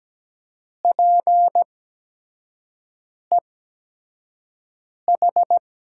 Morse code, heard as a keyed tone, about 17 wpm, spaced as for 5 wpm, 700 Hz. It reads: PEH
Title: USA Call Signs - 5 Characters - 8x Character Spacing - 17wpm